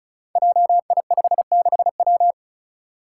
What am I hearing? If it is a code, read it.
Morse code, heard as a keyed tone, about 36 wpm, 700 Hz. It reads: JI56W
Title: Five Challenging Letter-Number Combinations 35wpm